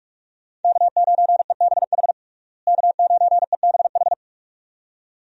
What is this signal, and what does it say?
Morse code, heard as a keyed tone, about 46 wpm, 700 Hz. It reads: K9EBH K9EBH
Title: USA Call Signs - 5 Characters - Repeated 2x - 45wpm